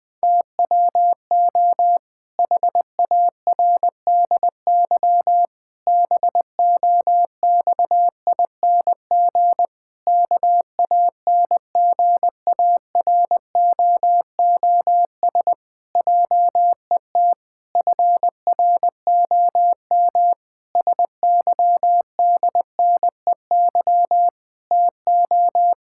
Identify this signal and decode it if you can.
Morse code, heard as a keyed tone, about 20 wpm, 700 Hz. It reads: TWO HARDY BOXING KANGAROOS JET FROM SYDNEY TO
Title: Pangram Challenge v2 - 20wpm